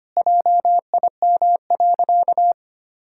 Morse code, heard as a keyed tone, about 25 wpm, 700 Hz. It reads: JIM.
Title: Pangram Challenge v2 - 25wpm